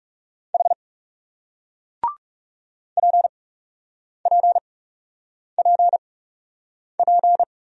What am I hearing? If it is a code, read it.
Morse code, heard as a keyed tone, about 40 wpm, 700 Hz. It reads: H P P P P